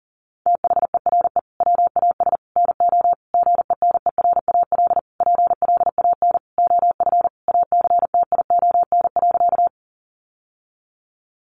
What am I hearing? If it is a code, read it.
Morse code, heard as a keyed tone, about 40 wpm, 700 Hz. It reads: THERE WAS NO GENERAL PLAN OF ACTION.
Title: Sentences from Top 700 Words 40wpm